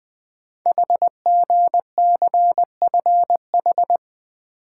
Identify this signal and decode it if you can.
Morse code, heard as a keyed tone, about 20 wpm, 700 Hz. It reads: HGCFH
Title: Code Groups - 5 Letters - 20wpm